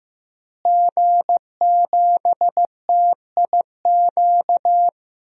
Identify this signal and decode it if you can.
Morse code, heard as a keyed tone, about 15 wpm, 700 Hz. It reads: G7TIQ
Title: DXCC-100 Call Signs - 5 Characters - Repeated 3x - 15wpm